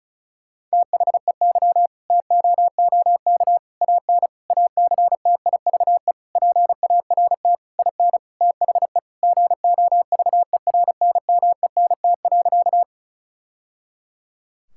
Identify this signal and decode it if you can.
Morse code, heard as a keyed tone, about 36 wpm, 700 Hz. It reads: THEY TOOK AN ACTIVE PART IN THE GOVERNMENT.
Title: Sentences from Top 1000 Words - Review Part 6 - 35wpm